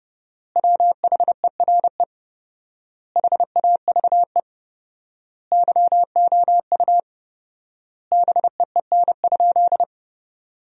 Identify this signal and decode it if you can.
Morse code, heard as a keyed tone, about 32 wpm, 700 Hz. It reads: WHERE HAVE YOU BEEN?
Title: 4x Word-Spacing - Sentences from Top 400 Words - 30wpm